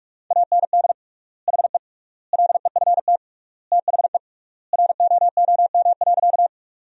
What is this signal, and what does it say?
Morse code, heard as a keyed tone, about 46 wpm, 700 Hz. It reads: AND HE LEFT THE ROOM.
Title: 3x Word-Spacing - Sentences from Top 300 Words - 45wpm